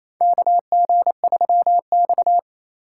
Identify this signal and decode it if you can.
Morse code, heard as a keyed tone, about 28 wpm, 700 Hz. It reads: KG3X
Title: USA Call Signs - 4 Characters - 6x Character Spacing - 28wpm